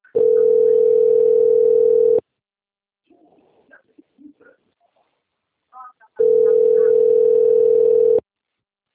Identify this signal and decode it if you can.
Ringback tone (North America), 440 + 480 Hz.
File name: out- - - - - .35590